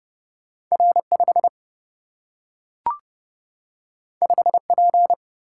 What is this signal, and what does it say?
Morse code, heard as a keyed tone, about 30 wpm, 700 Hz. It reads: R5 5P
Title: Single-Letter Single-Number 30wpm